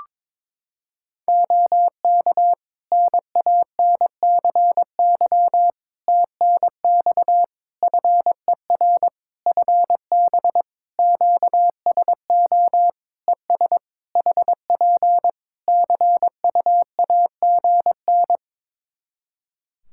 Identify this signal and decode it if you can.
Morse code, heard as a keyed tone, about 22 wpm, 700 Hz. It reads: OK NANCY TNX FER FB QSO ES HP CUAGN
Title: Practice QSO - Exchange 4 - Ending - 22wpm